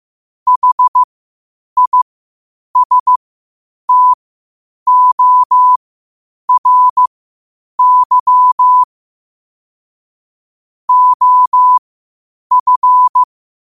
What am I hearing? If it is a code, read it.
Morse code, heard as a keyed tone, about 15 wpm, spaced as for 8 wpm, 1000 Hz. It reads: HISTORY OF
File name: qotdfiles / 8wpm